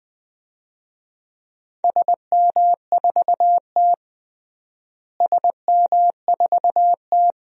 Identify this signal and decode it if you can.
Morse code, heard as a keyed tone, about 20 wpm, 700 Hz. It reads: SM4T SM4T
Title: DXCC-100 Call Signs - 4 Characters - Repeated 3x - 20wpm